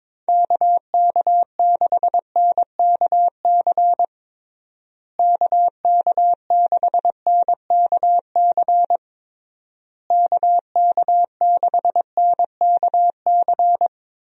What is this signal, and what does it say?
Morse code, heard as a keyed tone, about 22 wpm, 700 Hz. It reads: KK6NKC KK6NKC KK6NKC
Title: USA Call Signs - 6 Characters - Repeated 3x - 22wpm